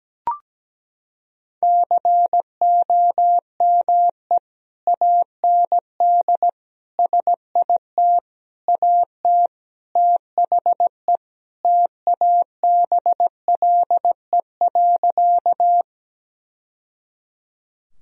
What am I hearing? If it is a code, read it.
Morse code, heard as a keyed tone, about 17 wpm, 700 Hz. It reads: COME AND SIT AT THE TABLE.
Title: Sentences from Top 1000 Words - Review Part 2 - 17wpm